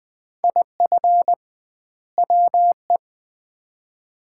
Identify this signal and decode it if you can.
Morse code, heard as a keyed tone, about 20 wpm, 700 Hz. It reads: IF WE
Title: 2x Word-Spacing - Sets of 2 Words - 20wpm